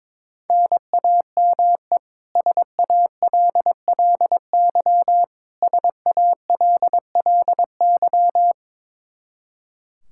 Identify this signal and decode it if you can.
Morse code, heard as a keyed tone, about 22 wpm, 700 Hz. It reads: NAME SALLY SALLY
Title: K1USN SST QSO Practice 22wpm